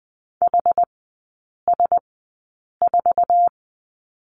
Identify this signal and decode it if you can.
Morse code, heard as a keyed tone, about 20 wpm, spaced as for 8 wpm, 700 Hz. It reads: HS4
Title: 121 Lesson 20e - Number 3 - Three Characters 20wpm